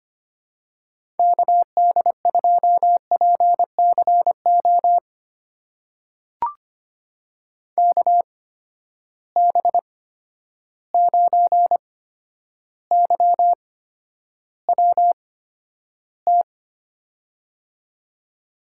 Morse code, heard as a keyed tone, about 25 wpm, spaced as for 11 wpm, 700 Hz. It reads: KD2PCO KB9YWT
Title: USA Call Signs - 6 Characters - 8x Character Spacing - 25wpm